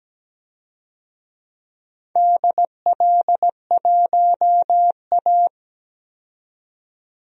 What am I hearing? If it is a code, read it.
Morse code, heard as a keyed tone, about 17 wpm, 700 Hz. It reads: DL1A
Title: DXCC-100 Call Signs - 4 Characters - 17wpm